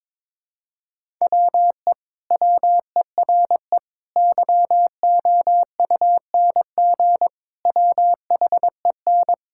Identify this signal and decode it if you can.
Morse code, heard as a keyed tone, about 22 wpm, 700 Hz. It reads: WE WERE YOUNG WHEN
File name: practice / 22wpm / Sentences with 2-Word Phrases - 22wpm